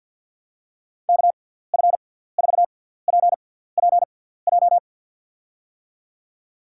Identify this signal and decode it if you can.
Morse code, heard as a keyed tone, about 52 wpm, spaced as for 17 wpm, 700 Hz. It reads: KF4PPJ
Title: USA Call Signs - 6 Characters - 6x Character Spacing - 50wpm